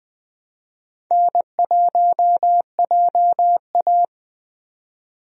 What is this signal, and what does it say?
Morse code, heard as a keyed tone, about 20 wpm, 700 Hz. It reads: N1JA
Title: USA Call Signs - 4 Characters - 20wpm